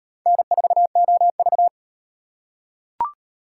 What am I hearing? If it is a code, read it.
Morse code, heard as a keyed tone, about 38 wpm, 700 Hz. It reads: N4OV